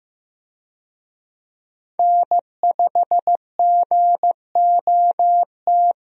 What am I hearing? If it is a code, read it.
Morse code, heard as a keyed tone, about 15 wpm, 700 Hz. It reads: N5GOT
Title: USA Call Signs - 5 Characters - Repeated 2x - 15wpm